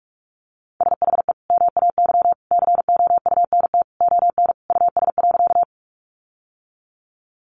Morse code, heard as a keyed tone, about 44 wpm, 700 Hz. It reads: SHE MAY COUNT ON US.
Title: Sentences from Top 1500 Words 45wpm